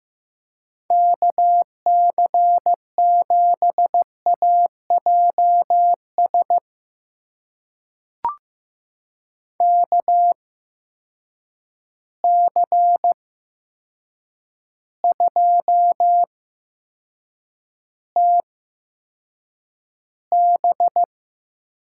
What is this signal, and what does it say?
Morse code, heard as a keyed tone, about 15 wpm, spaced as for 7 wpm, 700 Hz. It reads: KC7AJS KC2TB
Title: USA Call Signs - 6 Characters - 8x Character Spacing - 15wpm